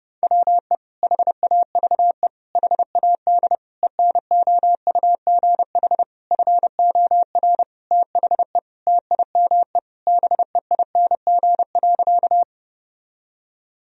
Morse code, heard as a keyed tone, about 30 wpm, 700 Hz. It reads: WE HAVE HAD ENOUGH FOR THE TIME BEING.
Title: Sentences from Top 1000 Words - Review Part 3 - 30wpm